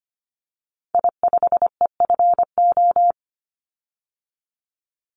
Morse code, heard as a keyed tone, about 25 wpm, 700 Hz. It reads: I5EFO
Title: Call Signs - 25wpm